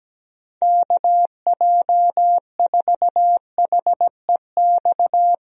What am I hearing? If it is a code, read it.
Morse code, heard as a keyed tone, about 17 wpm, 700 Hz. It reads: KJ4HEX
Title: USA Call Signs - 6 Characters - Repeated 2x - 17wpm